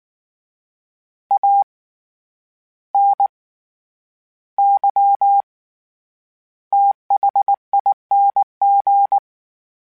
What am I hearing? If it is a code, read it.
Morse code, heard as a keyed tone, about 19 wpm, spaced as for 10 wpm, 800 Hz. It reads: ANYTHING
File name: Special INGs / Special INGs-mp3s / SlowToFaster ing @18